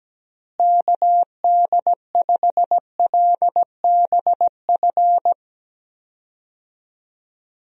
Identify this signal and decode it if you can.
Morse code, heard as a keyed tone, about 17 wpm, 700 Hz. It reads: KD5LBF
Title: USA Call Signs - 6 Characters - Repeated 3x - 17wpm